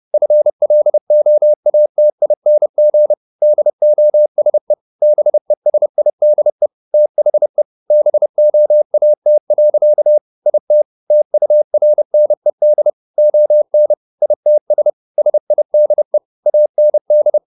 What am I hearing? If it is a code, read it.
Morse code, heard as a keyed tone, about 30 wpm, 600 Hz. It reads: FLOATING DOSE BESIDE THE BOAT. IT TURNED ON ITS SIDE AND
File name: the old man and the sea / the old man and the sea 1 30wpm